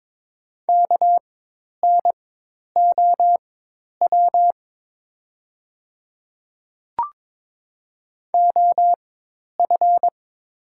Morse code, heard as a keyed tone, about 22 wpm, spaced as for 10 wpm, 690 Hz. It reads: KNOW OF
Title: 4x Character Spacing - Top 100 Words - 22wpm